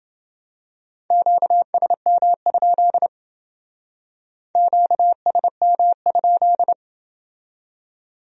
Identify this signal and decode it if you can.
Morse code, heard as a keyed tone, about 31 wpm, 700 Hz. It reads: QSM? QSM?